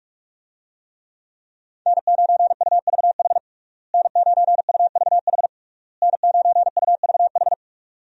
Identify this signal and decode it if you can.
Morse code, heard as a keyed tone, about 46 wpm, 700 Hz. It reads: N9UVH N9UVH N9UVH
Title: USA Call Signs - 5 Characters - Repeated 3x - 45wpm